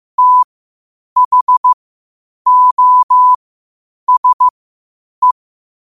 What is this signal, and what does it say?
Morse code, heard as a keyed tone, about 15 wpm, spaced as for 8 wpm, 1000 Hz. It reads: THOSE
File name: qotdfiles / 8wpm